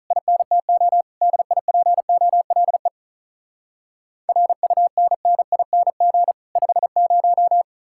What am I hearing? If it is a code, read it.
Morse code, heard as a keyed tone, about 48 wpm, 700 Hz. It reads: INTO DIPOLE RUNNING 50